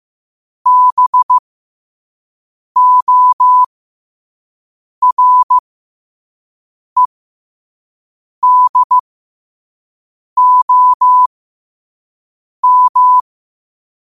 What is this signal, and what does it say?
Morse code, heard as a keyed tone, about 15 wpm, spaced as for 5 wpm, 1000 Hz. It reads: BOREDOM
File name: qotdfiles / 5wpm